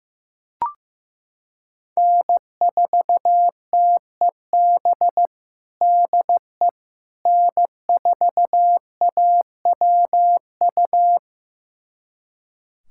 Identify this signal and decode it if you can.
Morse code, heard as a keyed tone, about 16 wpm, 700 Hz. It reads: N4TEB DE N4AWU
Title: USA Callsign DE Callsign - 5 Characters - 15wpm